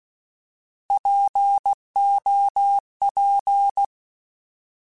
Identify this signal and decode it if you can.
Morse code, heard as a keyed tone, about 16 wpm, 800 Hz. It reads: POP
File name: Alphabetical / Alphabetical mp3s / Ps @15